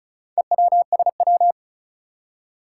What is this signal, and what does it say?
Morse code, heard as a keyed tone, about 36 wpm, 690 Hz. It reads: EWSW
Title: Four Challenging Letter-Number Combinations 35wpm